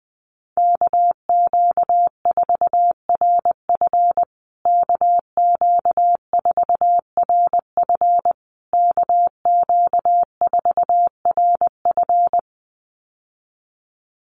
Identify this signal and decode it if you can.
Morse code, heard as a keyed tone, about 20 wpm, 700 Hz. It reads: KQ4RF KQ4RF KQ4RF
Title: USA Call Signs Repeated 3x 20wpm